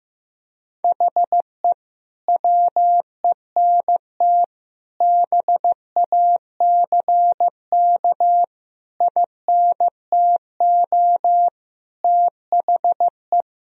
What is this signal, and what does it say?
Morse code, heard as a keyed tone, about 15 wpm, 700 Hz. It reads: HE WENT BACK INTO THE
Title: Sentences from Top 1000 Words - Review Part 2 - 15wpm